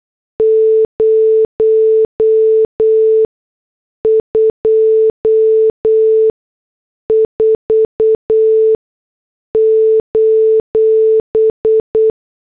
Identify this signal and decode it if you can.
Morse code, heard as a keyed tone, about 8 wpm, spaced as for 6 wpm, 440 Hz. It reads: 024:
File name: morse / headlines